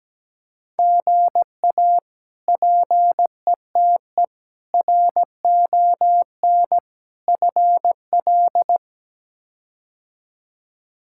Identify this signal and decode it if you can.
Morse code, heard as a keyed tone, about 17 wpm, 700 Hz. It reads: GA PETE RON FL
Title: K1USN SST QSO Practice - 17wpm